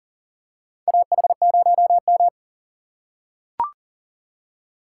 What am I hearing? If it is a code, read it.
Morse code, heard as a keyed tone, about 37 wpm, 700 Hz. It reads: AH0M
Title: USA Call Signs - 4 Characters - 2x Character Spacing - 40wpm